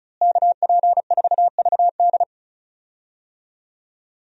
Morse code, heard as a keyed tone, about 34 wpm, 700 Hz. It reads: KP4VD
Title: DXCC-100 Call Signs - 5 Characters - Repeated 3x - 35wpm